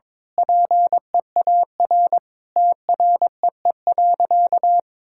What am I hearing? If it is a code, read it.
Morse code, heard as a keyed tone, about 22 wpm, 700 Hz. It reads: PEAR TREE.
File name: Sentences with 9-Word Phrases - 22wpm